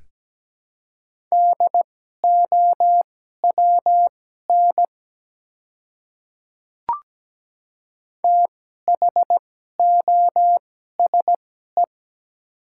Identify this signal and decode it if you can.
Morse code, heard as a keyed tone, about 17 wpm, spaced as for 12 wpm, 700 Hz. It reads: DOWN THOSE
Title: 2x Character Spacing - Top 300 Words - 17wpm